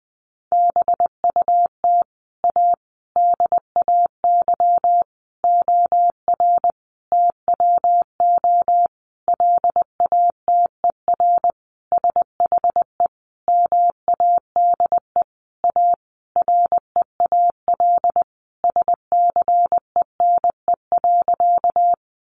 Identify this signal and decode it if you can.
Morse code, heard as a keyed tone, about 20 wpm, 700 Hz. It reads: BUT A DAY OR TWO LATER SHE MADE A REAL SCENE.
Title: Sentences from Top 1100 Words 20wpm